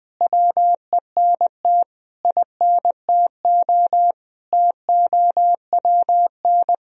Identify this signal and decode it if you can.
Morse code, heard as a keyed tone, about 20 wpm, 700 Hz. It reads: WENT INTO TOWN
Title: Sentences from Top 1000 Words - Review Part 2 - 20wpm